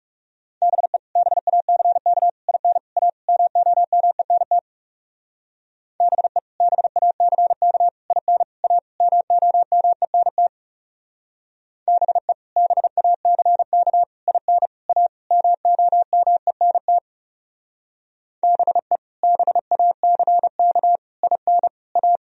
Morse code, heard as a keyed tone, about 37 wpm, 700 Hz. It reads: BE BACK IN A MOMENT BE BACK IN A MOMENT BE BACK IN A MOMENT BE BACK IN A
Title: Speed-Racing - Sets of 5 Words 30wpm